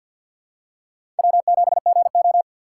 Morse code, heard as a keyed tone, about 53 wpm, 700 Hz. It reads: W7GO
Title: USA Call Signs - 4 Characters - 6x Character Spacing - 50wpm